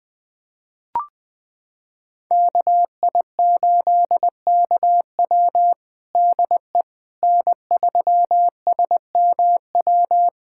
Morse code, heard as a keyed tone, about 21 wpm, 700 Hz. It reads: KI8KW DE N3SMW
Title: USA Callsign DE Callsign - 5 Characters - 20wpm